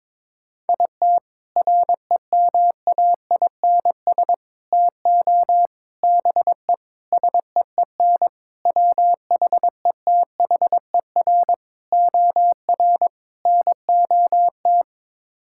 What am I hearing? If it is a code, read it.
Morse code, heard as a keyed tone, about 22 wpm, 700 Hz. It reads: IT REMAINS TO BE SEEN WHETHER OR NOT
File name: practice / 22wpm / Sets of 8 Words - 22wpm